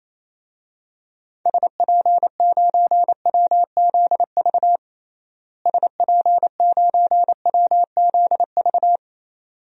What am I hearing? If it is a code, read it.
Morse code, heard as a keyed tone, about 28 wpm, 700 Hz. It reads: SP9WZV SP9WZV
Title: DXCC-100 Call Signs - 6 Characters - Repeated 2x - 28wpm